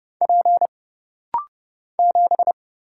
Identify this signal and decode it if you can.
Morse code, heard as a keyed tone, about 30 wpm, 700 Hz. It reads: P 7
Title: Single Letter-Number - Rapid-Fire 30wpm